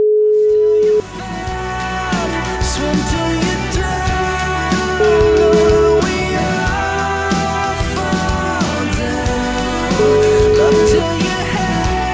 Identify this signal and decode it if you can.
Ringback tone (Europe, 425 Hz), with noise.